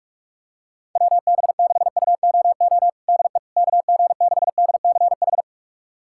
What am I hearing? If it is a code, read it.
Morse code, heard as a keyed tone, about 46 wpm, 700 Hz. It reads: WB6UOO DE KG6DCH
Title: USA Callsign DE Callsign 45wpm